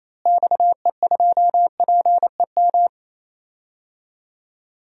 Morse code, heard as a keyed tone, about 28 wpm, 700 Hz. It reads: XE2PEM
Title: DXCC-100 Call Signs - 6 Characters - Repeated 3x - 28wpm